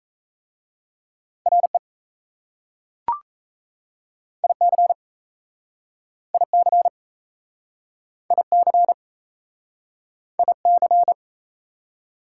Morse code, heard as a keyed tone, about 35 wpm, 690 Hz. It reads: RE IC IC IC IC
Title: Speed-Racing Prefixes and Suffixes 28wpm